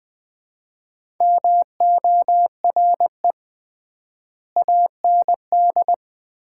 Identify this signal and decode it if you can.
Morse code, heard as a keyed tone, about 20 wpm, 700 Hz. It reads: MORE AND
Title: 3x Word-Spacing - Sets of 3 Words - 20wpm